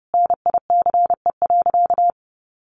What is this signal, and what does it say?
Morse code, heard as a keyed tone, about 30 wpm, 700 Hz. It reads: NICE.
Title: Sentences from Top 1200 Words 30wpm